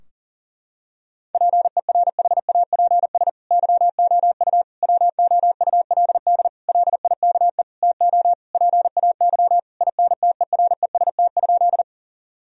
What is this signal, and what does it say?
Morse code, heard as a keyed tone, about 40 wpm, 700 Hz. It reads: PERHAPS YOU WOULD LIKE TO PAY INTEREST?
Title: Sentences from Top 1000 Words - Review Part 2 - 40wpm